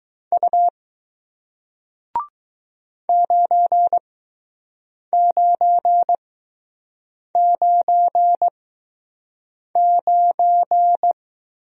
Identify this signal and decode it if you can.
Morse code, heard as a keyed tone, about 20 wpm, spaced as for 6 wpm, 700 Hz. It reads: U 9999